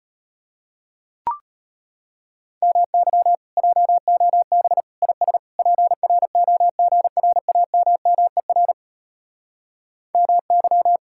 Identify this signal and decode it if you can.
Morse code, heard as a keyed tone, about 38 wpm, 700 Hz. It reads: MY JOB IS PROGRAMMER MY